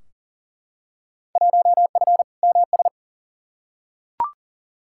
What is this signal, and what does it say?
Morse code, heard as a keyed tone, about 40 wpm, 700 Hz. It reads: 1F MS
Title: Field Day Exchanges Encore - 40wpm